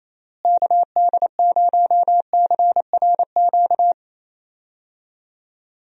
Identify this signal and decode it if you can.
Morse code, heard as a keyed tone, about 28 wpm, 700 Hz. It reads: KD0CRQ
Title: USA Call Signs - 6 Characters - Repeated 3x - 28wpm